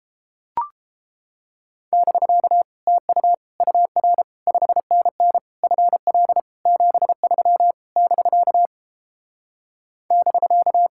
Morse code, heard as a keyed tone, about 29 wpm, 700 Hz. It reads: <BK> TU UR 5NN FL 73 <BK> <BK>